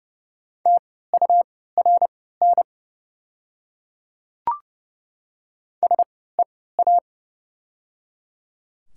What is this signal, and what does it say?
Morse code, heard as a keyed tone, about 30 wpm, spaced as for 17 wpm, 700 Hz. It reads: TURN SEA
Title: 3x Character Spacing - Top 300 Words - 30wpm